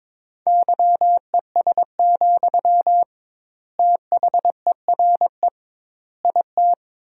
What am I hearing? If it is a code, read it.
Morse code, heard as a keyed tone, about 22 wpm, 700 Hz. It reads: YES, THERE IT
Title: 2x Word-Spacing - Sentences from Top 500 Words - 22wpm